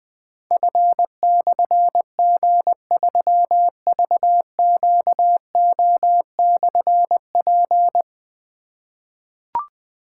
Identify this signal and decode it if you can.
Morse code, heard as a keyed tone, about 20 wpm, 700 Hz. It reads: F/G3VQO/P
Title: Difficult Call Signs 20wpm